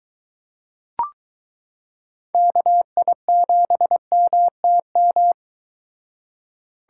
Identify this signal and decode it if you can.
Morse code, heard as a keyed tone, about 23 wpm, 700 Hz. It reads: KI7MTM